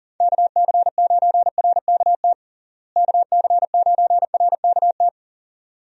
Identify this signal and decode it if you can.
Morse code, heard as a keyed tone, about 40 wpm, 700 Hz. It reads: KC9RKT KC9RKT
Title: USA Call Signs - 6 Characters - Repeated 2x - 40wpm